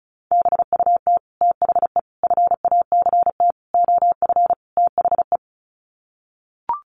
Morse code, heard as a keyed tone, about 34 wpm, 700 Hz. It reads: BUT THE FACT OF THE
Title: Sets of 5 Words Encore2 35wpm